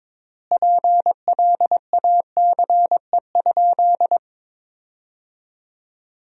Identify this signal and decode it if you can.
Morse code, heard as a keyed tone, about 21 wpm, 700 Hz. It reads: PLACE?
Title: 4x Word-Spacing - Sentences from Top 200 Words - 22wpm